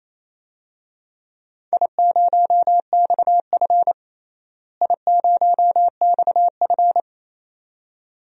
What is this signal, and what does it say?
Morse code, heard as a keyed tone, about 28 wpm, 700 Hz. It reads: I0XF I0XF
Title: DXCC-100 Call Signs - 4 Characters - Repeated 2x - 28wpm